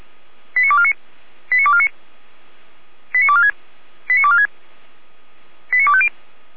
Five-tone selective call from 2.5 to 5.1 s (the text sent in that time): 89246 89246